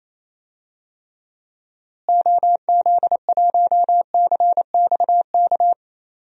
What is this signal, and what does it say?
Morse code, heard as a keyed tone, about 28 wpm, 700 Hz. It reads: OZ1CXK
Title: DXCC-100 Call Signs - 6 Characters - Repeated 3x - 28wpm